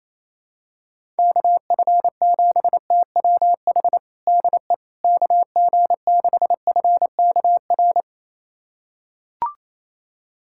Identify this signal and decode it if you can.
Morse code, heard as a keyed tone, about 28 wpm, 700 Hz. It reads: KF7TWH DE KG6FKR
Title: USA Callsign DE Callsign - 6 Characters - 28wpm